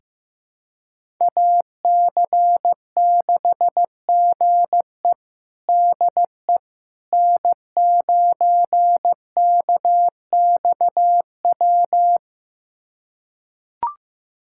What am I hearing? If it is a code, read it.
Morse code, heard as a keyed tone, about 15 wpm, 700 Hz. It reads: AC6GE DE N9KXW
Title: USA Callsign DE Callsign - 5 Characters - 15wpm